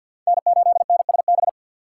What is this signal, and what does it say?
Morse code, heard as a keyed tone, about 53 wpm, 700 Hz. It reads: N8NSB
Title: USA Call Signs - 5 Characters - 4x Character Spacing - 50wpm